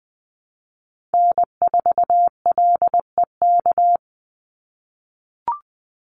Morse code, heard as a keyed tone, about 20 wpm, 700 Hz. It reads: N4LEK
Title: USA Call Signs 20wpm